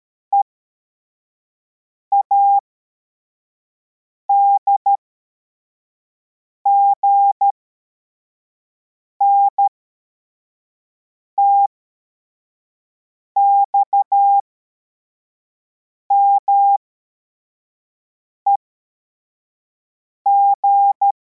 Morse code, heard as a keyed tone, about 13 wpm, spaced as for 4 wpm, 800 Hz. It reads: EADGN TXMEG
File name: formation / cw / Text33